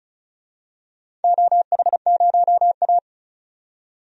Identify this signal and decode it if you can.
Morse code, heard as a keyed tone, about 36 wpm, 700 Hz. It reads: OH0A
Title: DXCC-100 Call Signs - 4 Characters - 35wpm